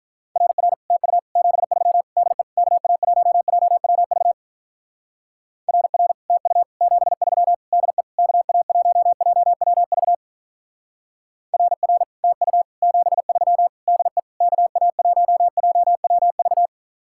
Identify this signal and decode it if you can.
Morse code, heard as a keyed tone, about 42 wpm, 700 Hz. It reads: RR TU 73 DE KA1JWV RR TU 73 DE KA1JWV RR TU 73 DE KA1JWV
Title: Speed-Racing Common QSO Phrases Encore - 35wpm